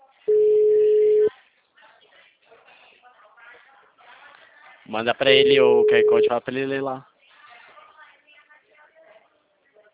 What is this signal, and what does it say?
Ringback tone (Europe, 425 Hz), with speech.